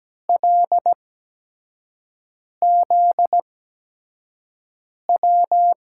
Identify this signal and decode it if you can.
Morse code, heard as a keyed tone, about 17 wpm, spaced as for 5 wpm, 700 Hz. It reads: LZW
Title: USA Call Signs - 6 Characters - 8x Character Spacing - 17wpm